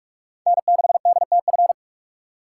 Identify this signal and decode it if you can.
Morse code, heard as a keyed tone, about 46 wpm, 700 Hz. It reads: N6DTF
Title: USA Call Signs - 2x Character Spacing - 45wpm